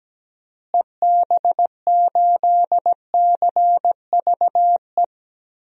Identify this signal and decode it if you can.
Morse code, heard as a keyed tone, about 17 wpm, 700 Hz. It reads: EB8CVE
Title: DXCC-100 Call Signs - 6 Characters - Repeated 3x - 17wpm